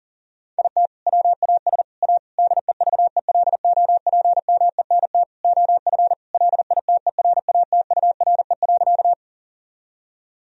Morse code, heard as a keyed tone, about 37 wpm, 700 Hz. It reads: IT WAS A DEVELOPMENT OF LITERATURE.
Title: Sentences from Top 1000 Words - Review Part 4 - 40wpm